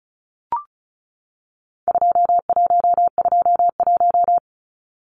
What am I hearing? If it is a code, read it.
Morse code, heard as a keyed tone, about 34 wpm, 700 Hz. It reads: 2121
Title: Numbers - 4-Digit 35wpm